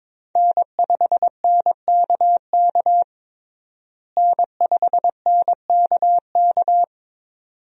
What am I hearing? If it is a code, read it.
Morse code, heard as a keyed tone, about 22 wpm, 700 Hz. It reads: N5NKK N5NKK
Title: USA Call Signs - 5 Characters - Repeated 2x - 22wpm